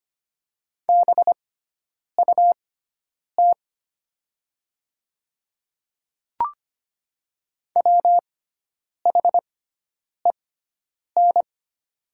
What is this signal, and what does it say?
Morse code, heard as a keyed tone, about 25 wpm, spaced as for 9 wpm, 700 Hz. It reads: BUT WHEN
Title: 6x Character Spacing - Top 100 Words - 25wpm